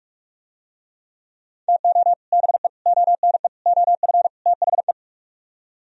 Morse code, heard as a keyed tone, about 45 wpm, 700 Hz. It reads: TO BE ONE OF THE